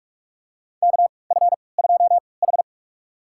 Morse code, heard as a keyed tone, about 46 wpm, spaced as for 26 wpm, 700 Hz. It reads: KF2H
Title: USA Call Signs - 4 Characters - 3x Character Spacing - 45wpm